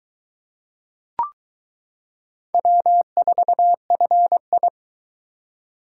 Morse code, heard as a keyed tone, about 22 wpm, 700 Hz. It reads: W4FI